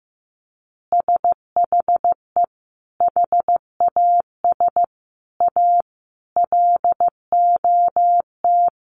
Morse code, heard as a keyed tone, about 15 wpm, 700 Hz. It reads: SHE HAS A LOT
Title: Additional Easy Phrases 15wpm